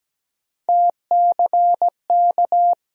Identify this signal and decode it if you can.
Morse code, heard as a keyed tone, about 17 wpm, 700 Hz. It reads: TCK
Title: USA Call Sign Postfixes - 3 Letters 17wpm